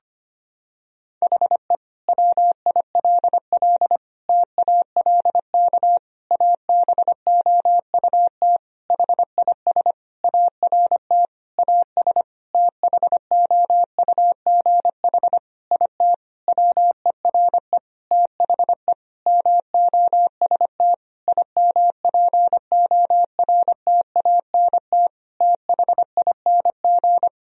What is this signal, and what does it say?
Morse code, heard as a keyed tone, about 25 wpm, 700 Hz. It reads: HE WILL TALK ABOUT HIS ART AS THOUGH IT WERE THE MOST IMPORTANT THING
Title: Sentences from Top 1000 Words - Review Part 3 - 25wpm